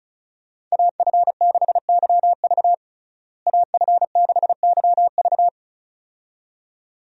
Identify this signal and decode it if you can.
Morse code, heard as a keyed tone, about 36 wpm, 700 Hz. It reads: AF6YV AF6YV
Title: USA Call Signs - 5 Characters - Repeated 2x - 35wpm